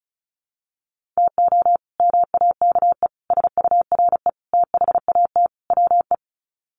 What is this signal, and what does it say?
Morse code, heard as a keyed tone, about 34 wpm, 700 Hz. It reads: TO MAKE SURE THAT WE
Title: Sets of 5 Words Encore 35wpm